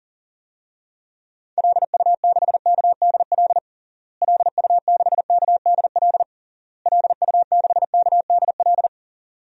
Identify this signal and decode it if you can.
Morse code, heard as a keyed tone, about 40 wpm, 700 Hz. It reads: LU6KDL LU6KDL LU6KDL
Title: DXCC-100 Call Signs - 6 Characters - Repeated 3x - 40wpm